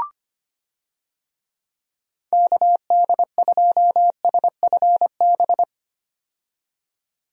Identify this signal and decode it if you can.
Morse code, heard as a keyed tone, about 25 wpm, 700 Hz. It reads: KD2SFB
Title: USA Call Signs - 6 Characters - 25wpm